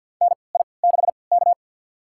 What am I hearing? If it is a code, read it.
Morse code, heard as a keyed tone, about 53 wpm, spaced as for 29 wpm, 700 Hz. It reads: NI6K
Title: USA Call Signs - 4 Characters - 3x Character Spacing - 50wpm